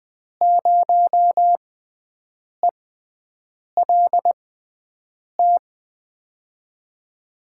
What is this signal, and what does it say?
Morse code, heard as a keyed tone, about 20 wpm, spaced as for 7 wpm, 700 Hz. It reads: 0ELT
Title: USA Call Signs - 5 Characters - 6x Character Spacing - 20wpm